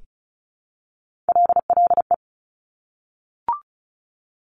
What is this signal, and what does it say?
Morse code, heard as a keyed tone, about 36 wpm, 700 Hz. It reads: LLE
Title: Top Three Letter Combinations in Words 35wpm